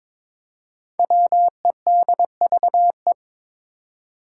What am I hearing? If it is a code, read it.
Morse code, heard as a keyed tone, about 22 wpm, 700 Hz. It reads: WEDVE
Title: Five Challenging Letter-Number Combinations 22wpm